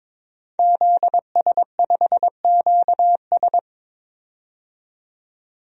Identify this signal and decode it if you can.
Morse code, heard as a keyed tone, about 22 wpm, 700 Hz. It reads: ZS5QS
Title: DXCC-100 Call Signs - 5 Characters - 22wpm